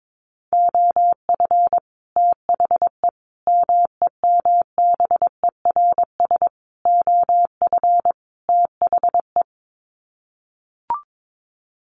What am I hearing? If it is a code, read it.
Morse code, heard as a keyed tone, about 22 wpm, 700 Hz. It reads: OF THE MEMBERS OF THE
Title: Sets of 5 Words Encore2 22wpm